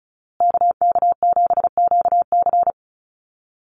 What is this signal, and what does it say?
Morse code, heard as a keyed tone, about 34 wpm, 700 Hz. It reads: KK7QC
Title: USA Call Signs Repeated 2x 35wpm